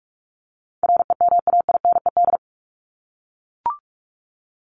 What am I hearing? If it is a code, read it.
Morse code, heard as a keyed tone, about 44 wpm, 700 Hz. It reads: REMAINED
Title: ED Suffix - Top 100 Words with Base Words 45wpm